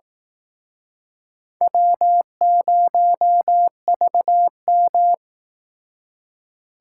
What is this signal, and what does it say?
Morse code, heard as a keyed tone, about 19 wpm, 700 Hz. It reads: W0VM
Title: Speed Racing - USA 1x2 Call Signs 15wpm